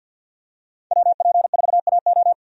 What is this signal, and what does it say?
Morse code, heard as a keyed tone, about 50 wpm, 700 Hz. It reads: WP4AO
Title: USA Call Signs - 5 Characters - 50wpm